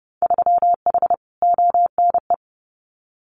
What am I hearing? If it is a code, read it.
Morse code, heard as a keyed tone, about 30 wpm, 700 Hz. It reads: 3H ONE
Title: Winter Field Day Exchanges Encore 30wpm